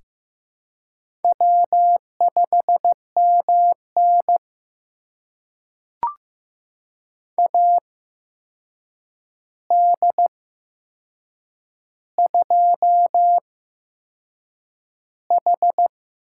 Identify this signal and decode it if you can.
Morse code, heard as a keyed tone, about 15 wpm, spaced as for 6 wpm, 700 Hz. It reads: W5MN AD2H
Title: USA Call Signs - 4 Characters - 8x Character Spacing - 15wpm